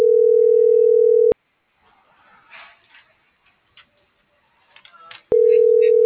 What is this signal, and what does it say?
Ringback tone (North America), 440 + 480 Hz.